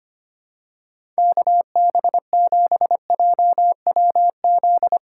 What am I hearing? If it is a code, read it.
Morse code, heard as a keyed tone, about 25 wpm, 700 Hz. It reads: KB7JWZ
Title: USA Call Signs - 6 Characters - 3x Character Spacing - 25wpm